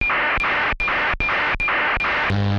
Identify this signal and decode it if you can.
ACARS aircraft data.